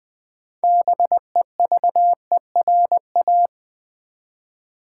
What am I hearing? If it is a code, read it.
Morse code, heard as a keyed tone, about 20 wpm, 700 Hz. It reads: BEVERA
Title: Build A Word - 3 Letters At A Time - 8-Letter Words - 20wpm